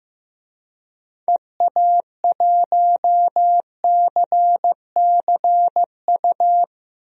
Morse code, heard as a keyed tone, about 15 wpm, 700 Hz. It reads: EA1CCU
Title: DXCC-100 Call Signs - 6 Characters - Repeated 3x - 15wpm